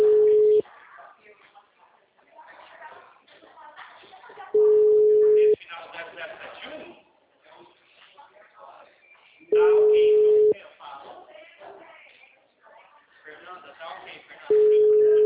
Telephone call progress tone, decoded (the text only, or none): ringback tone